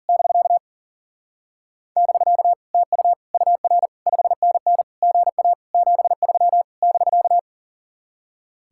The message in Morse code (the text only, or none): <BK> <BK> TU UR 5NN GA 73 <BK>